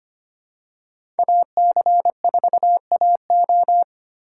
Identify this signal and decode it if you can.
Morse code, heard as a keyed tone, about 25 wpm, 700 Hz. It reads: AC4AO
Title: USA Call Signs - 5 Characters - 4x Character Spacing - 25wpm